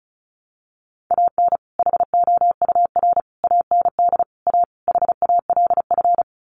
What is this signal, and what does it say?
Morse code, heard as a keyed tone, about 34 wpm, 700 Hz. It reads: AN HOUR AND A HALF
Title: Sets of 5 Words 35wpm